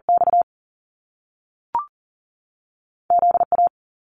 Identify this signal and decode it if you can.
Morse code, heard as a keyed tone, about 40 wpm, spaced as for 6 wpm, 700 Hz. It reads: XZA
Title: Top Two Letter Combinations in Words 40wpm